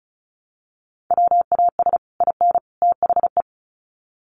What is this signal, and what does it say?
Morse code, heard as a keyed tone, about 34 wpm, 700 Hz. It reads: WAS IN THE
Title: Sets of 3 Words 35wpm